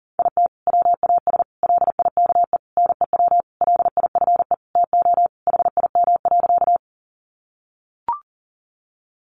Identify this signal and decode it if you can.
Morse code, heard as a keyed tone, about 40 wpm, 700 Hz. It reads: IT WAS LIKE NEW LIFE TO HIM.
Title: Sentences from Top 300 Words 40wpm